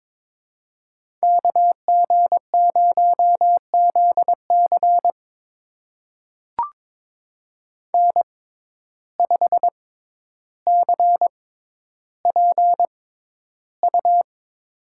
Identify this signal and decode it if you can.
Morse code, heard as a keyed tone, about 23 wpm, spaced as for 11 wpm, 700 Hz. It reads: KG0ZC N5CPU
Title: USA Call Signs - 5 Characters - 6x Character Spacing - 22wpm